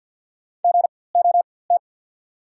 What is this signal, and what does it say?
Morse code, heard as a keyed tone, about 54 wpm, spaced as for 24 wpm, 690 Hz. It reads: GOT
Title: 4x Character Spacing - Top 500 Words - 50wpm